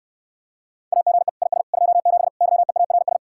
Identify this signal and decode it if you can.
Morse code, heard as a keyed tone, about 69 wpm, 700 Hz. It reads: AGE IS 27 YEARS